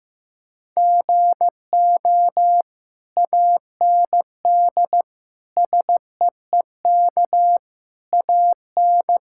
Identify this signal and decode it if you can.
Morse code, heard as a keyed tone, about 16 wpm, 700 Hz. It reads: GO AND SEEK AN
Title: Sentences from Top 1600 Words 15wpm